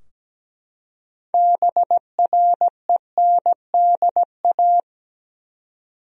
Morse code, heard as a keyed tone, about 17 wpm, 700 Hz. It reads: BRENDA
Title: USA Common Names - 17wpm